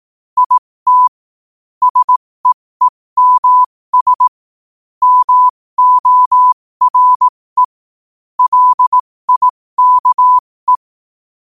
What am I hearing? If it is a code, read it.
Morse code, heard as a keyed tone, about 18 wpm, spaced as for 15 wpm, 1000 Hz. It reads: IT SEEMS MORE LIKE
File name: qotdfiles / 15wpm